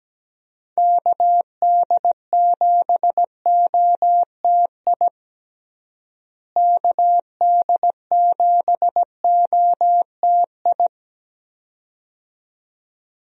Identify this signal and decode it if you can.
Morse code, heard as a keyed tone, about 17 wpm, 700 Hz. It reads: KD7OTI KD7OTI
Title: USA Call Signs - 6 Characters - Repeated 2x - 17wpm